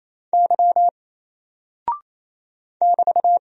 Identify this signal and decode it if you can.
Morse code, heard as a keyed tone, about 27 wpm, 700 Hz. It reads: Y <BT>
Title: Single Character (Letter:Number:Punctuation:Prosign) 28wpm